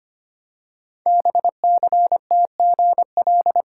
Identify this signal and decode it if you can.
Morse code, heard as a keyed tone, about 25 wpm, 700 Hz. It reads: BCTGL
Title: Code Groups - 5 Letters - 25wpm